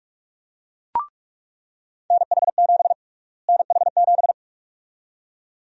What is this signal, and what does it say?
Morse code, heard as a keyed tone, about 42 wpm, 700 Hz. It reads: NH7 NH7
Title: USA Call Sign Prefixes - Repeated 2x - 45wpm